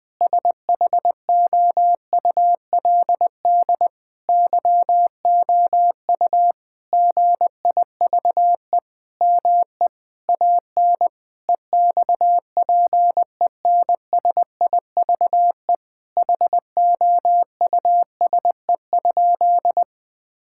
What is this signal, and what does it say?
Morse code, heard as a keyed tone, about 20 wpm, 700 Hz. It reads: SHOULD YOU GIVE ME AN EXPENSIVE HOUSE?
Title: Sentences from Top 1000 Words - Review Part 6 - 20wpm